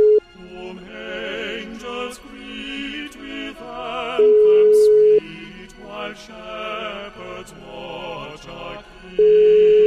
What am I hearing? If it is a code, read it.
Ringback tone (Europe, 425 Hz).